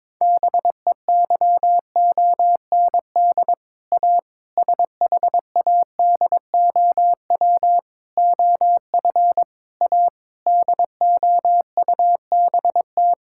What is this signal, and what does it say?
Morse code, heard as a keyed tone, about 22 wpm, 700 Hz. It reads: BEYOND A SHADOW OF A DOUBT
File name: practice / 22wpm / Sets of 7 Words - 22wpm